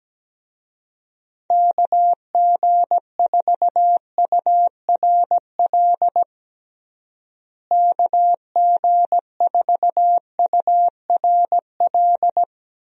Morse code, heard as a keyed tone, about 17 wpm, 700 Hz. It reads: KG4URL KG4URL
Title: USA Call Signs - 6 Characters - Repeated 2x - 17wpm